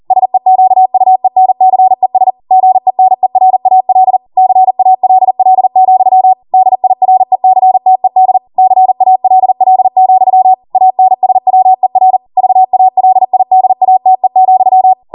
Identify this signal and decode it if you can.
Morse code, heard as a keyed tone, about 40 wpm, 750 Hz. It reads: SEQUENCES GENERAL CALL, DIRECTED CALL, ANSWER VALIDATE,